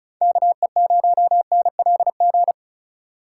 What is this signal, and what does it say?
Morse code, heard as a keyed tone, about 36 wpm, 700 Hz. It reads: KE0NLG
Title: USA Call Signs - 6 Characters - 4x Character Spacing - 35wpm